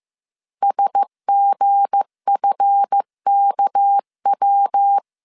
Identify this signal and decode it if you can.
Morse code, heard as a keyed tone, about 15 wpm, 800 Hz. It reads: SGFKW
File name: CW / DET / 1 9 1500C S C 14wpm 3245